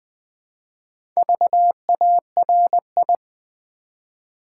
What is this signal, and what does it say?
Morse code, heard as a keyed tone, about 20 wpm, 700 Hz. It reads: VARI
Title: Build A Word - 2 Letters At A Time - 6-Letter Words - 20wpm